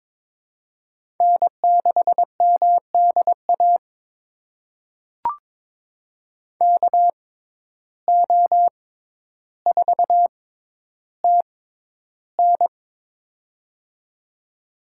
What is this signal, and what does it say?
Morse code, heard as a keyed tone, about 22 wpm, spaced as for 11 wpm, 700 Hz. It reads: N6MDA KO4TN
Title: USA Call Signs - 6x Character Spacing - 22wpm